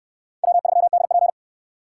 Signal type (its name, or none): Morse code